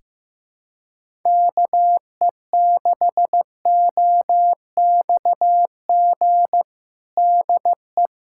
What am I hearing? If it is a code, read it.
Morse code, heard as a keyed tone, about 15 wpm, 700 Hz. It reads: KE6OXG DE
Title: USA Callsign DE Callsign - 6 Characters - 15wpm